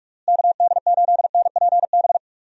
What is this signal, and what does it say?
Morse code, heard as a keyed tone, about 46 wpm, 700 Hz. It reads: KD8NPB
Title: USA Call Signs - 6 Characters - Repeated 3x - 45wpm